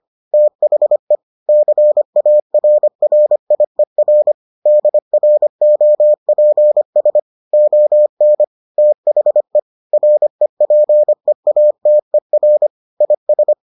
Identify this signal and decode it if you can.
Morse code, heard as a keyed tone, about 25 wpm, 600 Hz. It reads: THE CARRIER DROPS ON THE REPEATER IS